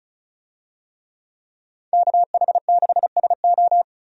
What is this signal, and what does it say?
Morse code, heard as a keyed tone, about 34 wpm, 700 Hz. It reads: KH6SO
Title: DXCC-100 Call Signs - 5 Characters - 35wpm